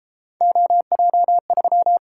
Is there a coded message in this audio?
Yes: Morse code from 0.4 to 2.0 s, heard as a keyed tone, about 31 wpm, 700 Hz. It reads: OJ3